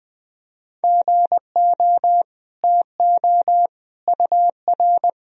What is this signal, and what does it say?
Morse code, heard as a keyed tone, about 20 wpm, 700 Hz. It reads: GO TO UR
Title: Sentences from Top 300 Words and Abbreviations 20wpm